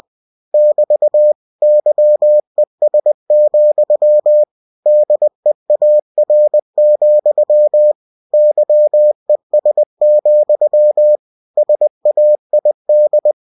Morse code, heard as a keyed tone, about 20 wpm, 600 Hz. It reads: <BT> YES, DEAR, YES, SAID
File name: cw-books / winnie-the-pooh / winnie-20wpm-page-0072